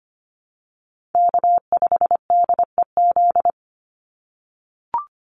Morse code, heard as a keyed tone, about 25 wpm, 700 Hz. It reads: K5DEZ
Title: NA SOTA Chaser Callsigns 25wpm